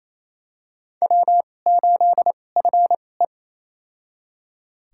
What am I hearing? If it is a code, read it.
Morse code, heard as a keyed tone, about 28 wpm, spaced as for 20 wpm, 700 Hz. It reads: W8FE
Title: USA Call Signs - 4 Characters - 2x Character Spacing - 28wpm